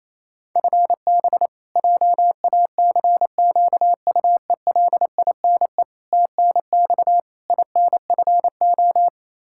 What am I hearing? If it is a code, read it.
Morse code, heard as a keyed tone, about 28 wpm, 700 Hz. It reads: FB JACQUELINE TNX INFO
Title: Practice QSO - Exchange 3 - More Info - 28wpm